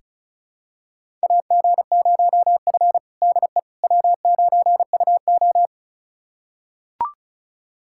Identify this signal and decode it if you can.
Morse code, heard as a keyed tone, about 36 wpm, 700 Hz. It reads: AG0F DE W9UO
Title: USA Callsign DE Callsign - 4 Characters - 35wpm